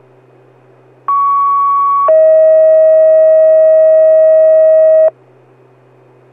Two-tone page, heard as a one-second tone, then a three-second tone, about 1120 then 640 Hz. It is heard from 1.1 to 5.1 s.